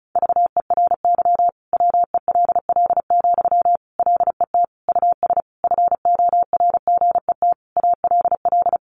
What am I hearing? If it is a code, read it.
Morse code, heard as a keyed tone, about 34 wpm, 700 Hz. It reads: VERY WELL, LET US FORGET ALL
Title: Sentences from Top 1200 Words 35wpm